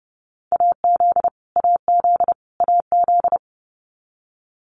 Morse code, heard as a keyed tone, about 30 wpm, 700 Hz. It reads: AZ AZ AZ
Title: US States Repeated 3x 30wpm